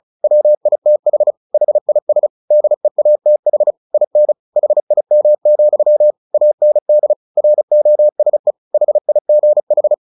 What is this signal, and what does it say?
Morse code, heard as a keyed tone, about 34 wpm, 600 Hz. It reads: WITH HIS DEATH IN HIM, AND ROSE HIGH
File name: the old man and the sea 2 35wpm